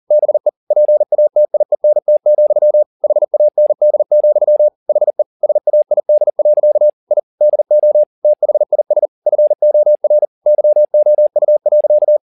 Morse code, heard as a keyed tone, about 40 wpm, 600 Hz. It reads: BE PATIENT, HAND, HE SAID. I DO THIS FOR YOU.